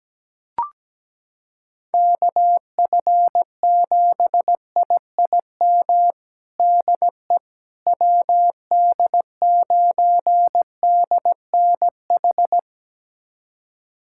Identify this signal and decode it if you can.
Morse code, heard as a keyed tone, about 17 wpm, 700 Hz. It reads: KF7IIM DE WD9DNH
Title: USA Callsign DE Callsign - 6 Characters - 17wpm